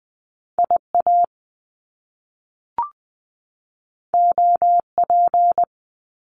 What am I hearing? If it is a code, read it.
Morse code, heard as a keyed tone, about 20 wpm, 700 Hz. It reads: IA OP
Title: Top Two Letter Combinations in Words 20wpm